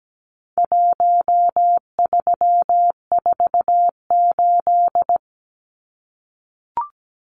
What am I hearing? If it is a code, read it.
Morse code, heard as a keyed tone, about 17 wpm, 700 Hz. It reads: 1348
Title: Numbers - Contest Style - 4-Digit 17wpm